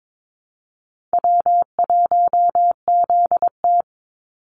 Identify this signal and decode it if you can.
Morse code, heard as a keyed tone, about 22 wpm, 700 Hz. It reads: W1ZT
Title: Easy Call Signs - 22wpm